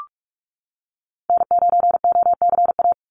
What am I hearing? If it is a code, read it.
Morse code, heard as a keyed tone, about 44 wpm, 700 Hz. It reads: N9OCA
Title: USA Call Signs 45wpm